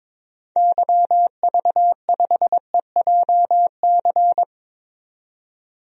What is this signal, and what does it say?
Morse code, heard as a keyed tone, about 21 wpm, 700 Hz. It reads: YV5EJC
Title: DXCC-100 Call Signs - 6 Characters - 22wpm